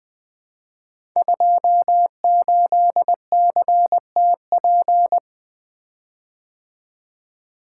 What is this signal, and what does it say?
Morse code, heard as a keyed tone, about 20 wpm, 700 Hz. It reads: 28CTP
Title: Code Groups - 5 Letter:Numbers - 20wpm